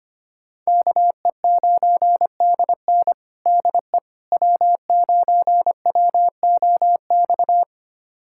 Morse code, heard as a keyed tone, about 25 wpm, 700 Hz. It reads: KE9DN DE W9WOX
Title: USA Callsign DE Callsign - 5 Characters - 25wpm